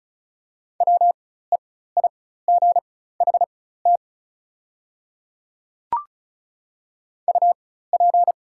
Morse code, heard as a keyed tone, about 36 wpm, spaced as for 16 wpm, 700 Hz. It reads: WEIGHT UP
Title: 4x Character Spacing - Top 500 Words - 35wpm